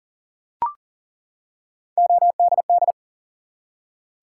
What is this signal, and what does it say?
Morse code, heard as a keyed tone, about 40 wpm, 700 Hz. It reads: ODD
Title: Three Letter Words - 40wpm